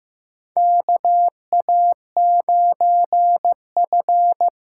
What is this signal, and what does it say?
Morse code, heard as a keyed tone, about 15 wpm, 700 Hz. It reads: KA9F
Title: USA Call Signs - 6 Characters - 6x Character Spacing - 15wpm